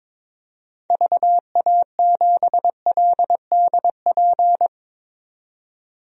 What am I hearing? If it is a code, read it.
Morse code, heard as a keyed tone, about 22 wpm, 700 Hz. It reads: VA7LDP
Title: DXCC-100 Call Signs - 6 Characters - 22wpm